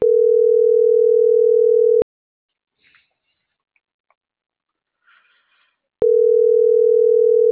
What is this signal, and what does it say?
Ringback tone (North America), 440 + 480 Hz.